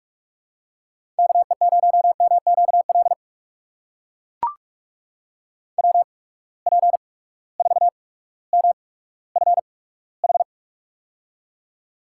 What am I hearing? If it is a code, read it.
Morse code, heard as a keyed tone, about 44 wpm, spaced as for 19 wpm, 700 Hz. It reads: KE0MQL WP4MFH
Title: USA Call Signs - 6 Characters - 8x Character Spacing - 45wpm